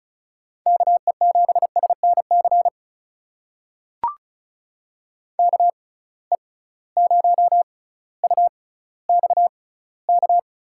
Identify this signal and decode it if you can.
Morse code, heard as a keyed tone, about 36 wpm, spaced as for 18 wpm, 700 Hz. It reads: KE7SNC KE0UXK
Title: USA Call Signs - 6 Characters - 6x Character Spacing - 35wpm